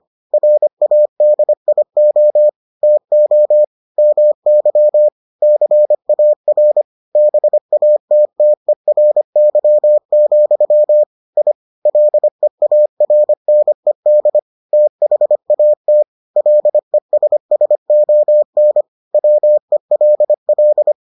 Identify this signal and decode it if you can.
Morse code, heard as a keyed tone, about 25 wpm, 600 Hz. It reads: RADIO TO MY CAR BATTERY, I LEARNED THAT LESSON WELL